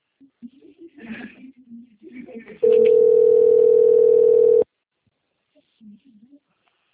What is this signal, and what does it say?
Ringback tone (North America), 440 + 480 Hz, with speech.
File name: rg-1600-unknown- - - .60386